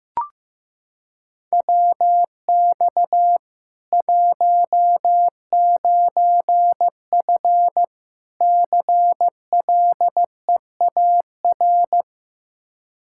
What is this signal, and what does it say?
Morse code, heard as a keyed tone, about 15 wpm, 700 Hz. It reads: WX 19F CLEAR
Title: Practice QSO - Exchange 2 - Info - 15wpm